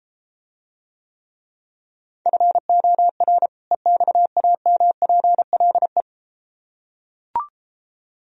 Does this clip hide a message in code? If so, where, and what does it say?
Yes: Morse code from 2.3 to 6.0 s, heard as a keyed tone, about 33 wpm, 700 Hz. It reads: FOR EXAMPLE